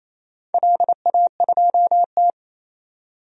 Morse code, heard as a keyed tone, about 27 wpm, 700 Hz. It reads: LA2T
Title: DXCC-100 Call Signs - 4 Characters - 28wpm